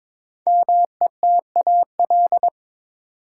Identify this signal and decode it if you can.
Morse code, heard as a keyed tone, about 22 wpm, 700 Hz. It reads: METAL
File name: 5 Letter Words - First 3 Letters Given - 22wpm